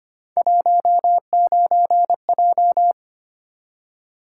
Morse code, heard as a keyed tone, about 26 wpm, 700 Hz. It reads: 19J
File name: Three Letter-Number Combinations - 25wpm